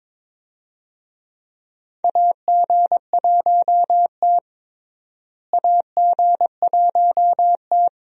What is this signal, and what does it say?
Morse code, heard as a keyed tone, about 22 wpm, 700 Hz. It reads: AG1T AG1T
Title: USA Call Signs - 4 Characters - Repeated 3x - 22wpm